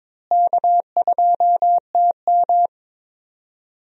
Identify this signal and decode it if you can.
Morse code, heard as a keyed tone, about 22 wpm, 700 Hz. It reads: K2TM
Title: USA Call Signs - 4 Characters - Repeated 2x - 22wpm